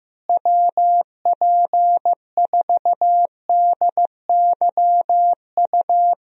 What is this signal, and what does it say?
Morse code, heard as a keyed tone, about 15 wpm, 700 Hz. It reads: WP4DYU
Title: USA Call Signs - 6 Characters - Repeated 2x - 15wpm